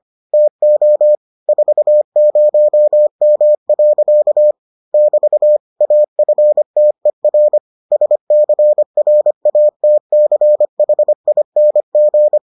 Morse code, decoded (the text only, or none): TO 40M. <BT> AFTER SCRATCHING